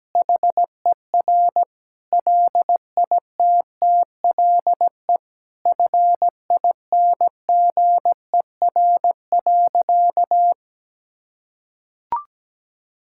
Morse code, heard as a keyed tone, about 17 wpm, 700 Hz. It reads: HER LITTLE FINGER.
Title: Sentences from Top 1600 Words 17wpm